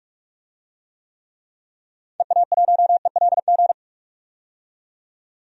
Morse code, heard as a keyed tone, about 46 wpm, 700 Hz. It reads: EA1ELG
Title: DXCC-100 Call Signs - 6 Characters - 45wpm